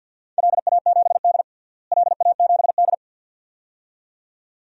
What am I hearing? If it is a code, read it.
Morse code, heard as a keyed tone, about 52 wpm, 700 Hz. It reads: LA7D LA7D
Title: DXCC-100 Call Signs - 4 Characters - Repeated 2x - 50wpm